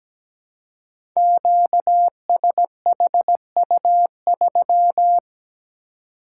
Morse code, heard as a keyed tone, about 17 wpm, 700 Hz. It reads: QSHU3
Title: Code Groups - 5 Letter:Numbers - 17wpm